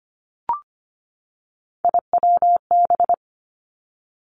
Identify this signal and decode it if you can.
Morse code, heard as a keyed tone, about 25 wpm, 700 Hz. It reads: IWB
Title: Three Challenging Letter-Number Combinations 25wpm